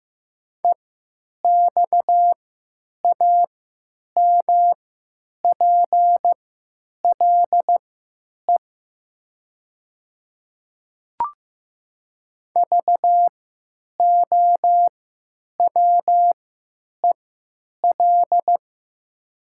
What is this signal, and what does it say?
Morse code, heard as a keyed tone, about 15 wpm, spaced as for 9 wpm, 700 Hz. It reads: EXAMPLE VOWEL
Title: 3x Character Spacing - Top 500 Words - 15wpm